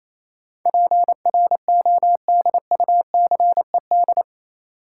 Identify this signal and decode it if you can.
Morse code, heard as a keyed tone, about 28 wpm, 700 Hz. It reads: PRODUCED
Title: Sentences from Top 1000 Words - Review Part 6 - New Word Drill - 28wpm